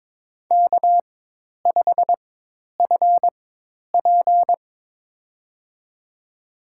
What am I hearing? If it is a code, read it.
Morse code, heard as a keyed tone, about 22 wpm, spaced as for 10 wpm, 700 Hz. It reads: K5FP
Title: USA Call Signs - 4 Characters - 4x Character Spacing - 22wpm